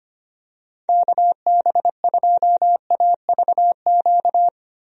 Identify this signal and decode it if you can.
Morse code, heard as a keyed tone, about 25 wpm, 700 Hz. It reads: KB2AVQ
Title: USA Call Signs - 6 Characters - 25wpm